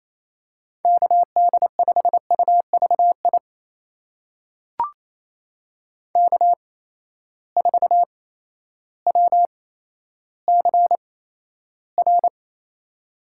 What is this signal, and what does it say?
Morse code, heard as a keyed tone, about 28 wpm, spaced as for 13 wpm, 700 Hz. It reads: KD5UVI K4WCR
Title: USA Call Signs - 8x Character Spacing - 28wpm